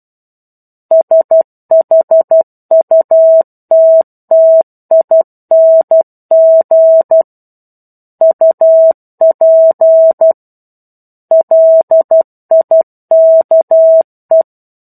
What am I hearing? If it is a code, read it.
Morse code, heard as a keyed tone, about 12 wpm, 650 Hz. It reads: SHUTTING UP LIKE
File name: 01-Chapter-01 12 WPM